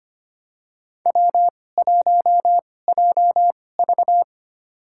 Morse code, heard as a keyed tone, about 25 wpm, spaced as for 18 wpm, 700 Hz. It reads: W1JV
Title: USA Call Signs - 4 Characters - 2x Character Spacing - 25wpm